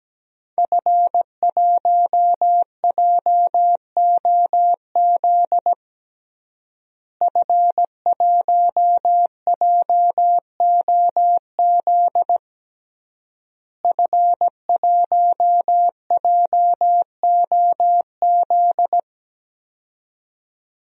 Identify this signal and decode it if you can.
Morse code, heard as a keyed tone, about 17 wpm, 700 Hz. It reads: F1JOZ F1JOZ F1JOZ
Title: DXCC-100 Call Signs - 5 Characters - Repeated 3x - 17wpm